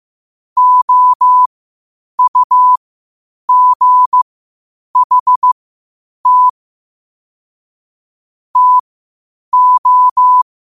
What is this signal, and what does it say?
Morse code, heard as a keyed tone, about 15 wpm, spaced as for 8 wpm, 1000 Hz. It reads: OUGHT TO
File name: qotdfiles / 8wpm